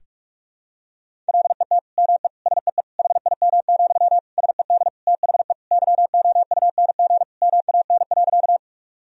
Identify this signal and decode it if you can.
Morse code, heard as a keyed tone, about 46 wpm, 700 Hz. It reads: LET ME SEE HIM, SED THE YOUNG MAN.
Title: Sentences from Top 400 Words and Abbreviations 45wpm